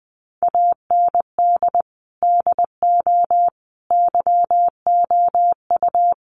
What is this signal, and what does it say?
Morse code, heard as a keyed tone, about 20 wpm, 700 Hz. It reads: AND DO YOU
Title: Sentences from Top 100 Words 20wpm